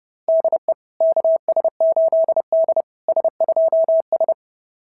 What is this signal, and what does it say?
Morse code, heard as a keyed tone, about 30 wpm, 690 Hz. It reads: DE KS8D S2S
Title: SOTA QSO Practice 30wpm